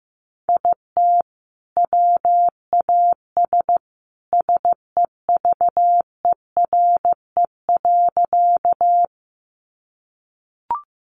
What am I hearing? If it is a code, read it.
Morse code, heard as a keyed tone, about 15 wpm, 700 Hz. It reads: IT WAS SEVERE.
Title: Sentences from Top 1400 Words 15wpm